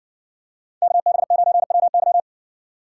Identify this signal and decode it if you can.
Morse code, heard as a keyed tone, about 60 wpm, 700 Hz. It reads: KB9WY